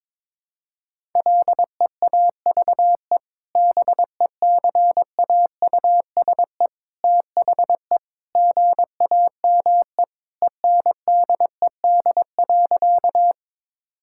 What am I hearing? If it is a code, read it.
Morse code, heard as a keyed tone, about 21 wpm, 700 Hz. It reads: LEAVE BECAUSE THE GAME ENDED.
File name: practice / 22wpm / Sentences with 2-Word Phrases - 22wpm